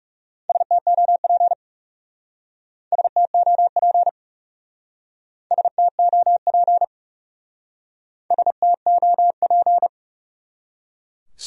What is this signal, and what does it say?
Morse code, heard as a keyed tone, about 35 wpm, 700 Hz. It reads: STOP STOP STOP STOP
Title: Speed-Racing Top 500 Words 30wpm